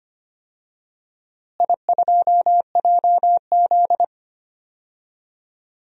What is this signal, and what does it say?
Morse code, heard as a keyed tone, about 25 wpm, 700 Hz. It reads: I2JZ
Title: DXCC-100 Call Signs - 4 Characters - Repeated 3x - 25wpm